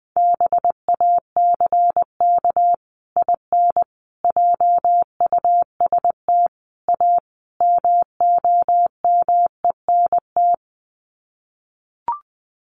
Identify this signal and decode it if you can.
Morse code, heard as a keyed tone, about 20 wpm, 700 Hz. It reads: BACK IN JUST A MOMENT
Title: Sets of 5 Words Encore 20wpm